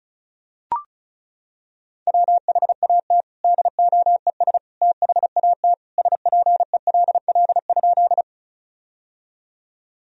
Morse code, heard as a keyed tone, about 34 wpm, 700 Hz. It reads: WHAT DOES THAT SPELL?
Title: Sentences from Top 1000 Words - Review Part 1 - 35wpm